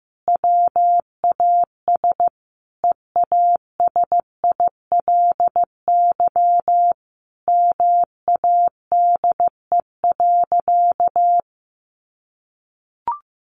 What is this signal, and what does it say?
Morse code, heard as a keyed tone, about 14 wpm, 700 Hz. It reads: WAS EASILY MADE.
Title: Sentences from Top 1200 Words 15wpm